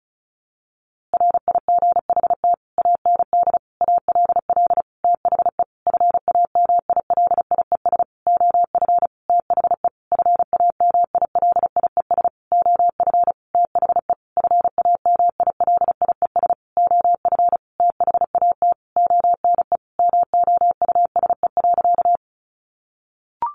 Morse code, heard as a keyed tone, about 34 wpm, 700 Hz. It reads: RIGHT AND ALL THE FAMILIES OF THE FAMILIES OF THE FAMILIES OF THAT ONE MOUSE.
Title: A Sound Of Thunder 35wpm